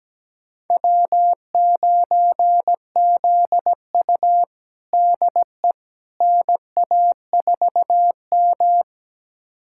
Morse code, heard as a keyed tone, about 17 wpm, 700 Hz. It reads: W9ZU DE NA4M
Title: USA Callsign DE Callsign - 4 Characters - 17wpm